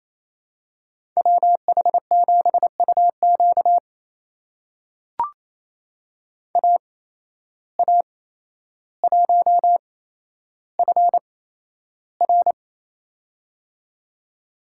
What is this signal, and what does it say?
Morse code, heard as a keyed tone, about 28 wpm, spaced as for 12 wpm, 700 Hz. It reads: WH7UQ AA1FR
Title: USA Call Signs - 5 Characters - 8x Character Spacing - 28wpm